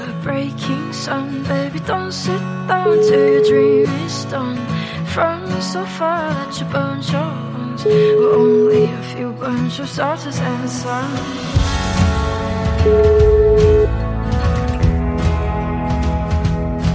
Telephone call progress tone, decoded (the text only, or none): ringback tone